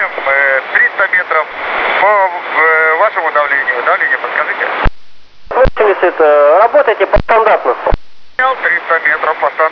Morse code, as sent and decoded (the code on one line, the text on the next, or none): none